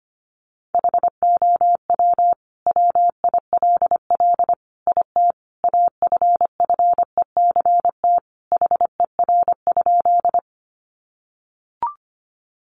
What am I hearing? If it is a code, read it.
Morse code, heard as a keyed tone, about 25 wpm, 700 Hz. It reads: HOW WILL IT AFFECT HER?
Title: Sentences from Top 900 Words 25wpm